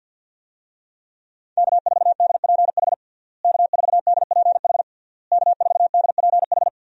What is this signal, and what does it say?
Morse code, heard as a keyed tone, about 52 wpm, 700 Hz. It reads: K4DPH K4DPH K4DPH
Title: USA Call Signs - 5 Characters - Repeated 3x - 50wpm